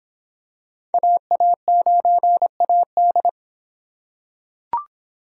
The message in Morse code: AA9AD

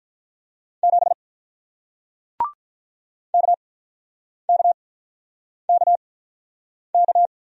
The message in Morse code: 7 KKKK

